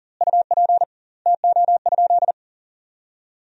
Morse code, heard as a keyed tone, about 40 wpm, 700 Hz. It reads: UP TO?
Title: 2x Word-Spacing - Sentences from Top 100 Words - 40wpm